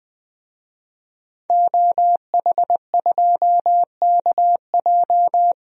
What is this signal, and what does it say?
Morse code, heard as a keyed tone, about 21 wpm, 700 Hz. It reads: OH2KJ
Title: DXCC-100 Call Signs - 5 Characters - Repeated 2x - 20wpm